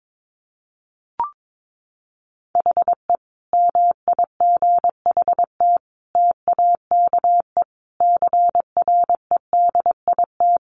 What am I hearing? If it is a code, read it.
Morse code, heard as a keyed tone, about 22 wpm, 700 Hz. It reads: HE MIGHT TAKE CREDIT
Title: Sentences from Top 1000 Words 22wpm